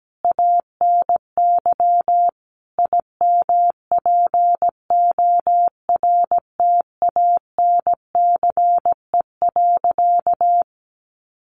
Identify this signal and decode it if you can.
Morse code, heard as a keyed tone, about 17 wpm, 700 Hz. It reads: ANY IMPORTANCE.
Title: Sentences from Top 1200 Words 17wpm